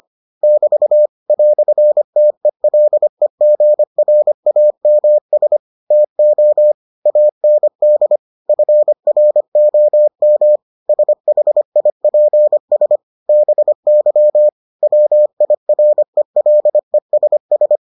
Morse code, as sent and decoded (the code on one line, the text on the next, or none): -...- / .-..-. - . .-.. --. .-. .- -- ... / - --- / .- -. -.. / ..-. .-. --- -- / ... .... .. .--. ... / -... -.-- / .-- .. .-. .-.. ... ...
<BT> "TELEGRAMS TO AND FROM SHIPS BY WIRELESS